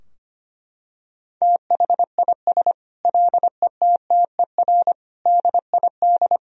Morse code, heard as a keyed tone, about 25 wpm, 700 Hz. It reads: THIS LETTER DID
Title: Sentences from Top 1000 Words - Review Part 4 - 25wpm